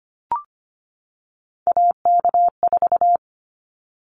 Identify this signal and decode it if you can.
Morse code, heard as a keyed tone, about 25 wpm, 700 Hz. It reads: AK4
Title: Callsign Prefixes for Top 25 DXCC - 25wpm